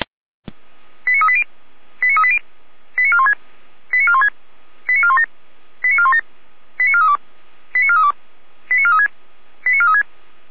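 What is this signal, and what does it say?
Five-tone selective call (ZVEI), heard as quick runs of five tones, tones of about 70 ms. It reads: 89380 89380 89416 89416 89417 89417 89432 89432 89436 89436
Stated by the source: Five-tone ZVEI selective call received on an SDR.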